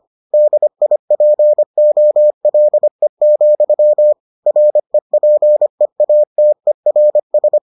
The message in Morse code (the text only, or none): DIPOLE, REPEATERS